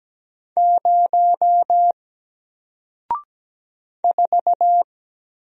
Morse code, heard as a keyed tone, about 17 wpm, 700 Hz. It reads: 0 4
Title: Single Letter-Number Alt 17wpm